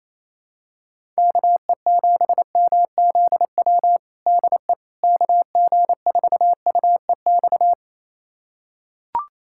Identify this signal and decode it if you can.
Morse code, heard as a keyed tone, about 27 wpm, 700 Hz. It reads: KE7MZW DE KG4UEX
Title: USA Callsign DE Callsign - 6 Characters - 28wpm